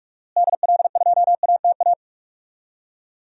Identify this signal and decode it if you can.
Morse code, heard as a keyed tone, about 42 wpm, 700 Hz. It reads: DL2ATA
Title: DXCC-100 Call Signs - 6 Characters - Repeated 2x - 45wpm